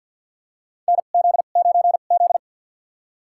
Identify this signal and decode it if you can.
Morse code, heard as a keyed tone, about 52 wpm, spaced as for 36 wpm, 700 Hz. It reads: NZ9Z
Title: USA Call Signs - 2x Character Spacing - 50wpm